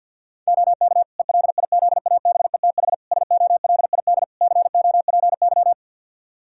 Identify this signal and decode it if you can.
Morse code, heard as a keyed tone, about 51 wpm, 700 Hz. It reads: OK ELIZABETH SOLID COPY
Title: Practice QSO - Exchange 3 - More Info - 50wpm